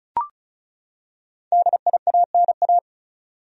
Morse code, heard as a keyed tone, about 36 wpm, 700 Hz. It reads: DIANA
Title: USA Common Names - Max 5 Letters - 35wpm